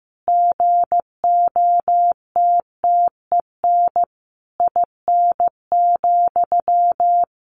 Morse code, heard as a keyed tone, about 15 wpm, 700 Hz. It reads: GOTTEN IN,
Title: A Sound Of Thunder 15wpm